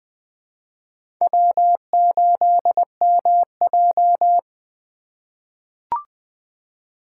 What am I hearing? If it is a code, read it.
Morse code, heard as a keyed tone, about 20 wpm, 700 Hz. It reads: W8MJ
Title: USA Call Signs - 4 Characters - 3x Character Spacing - 20wpm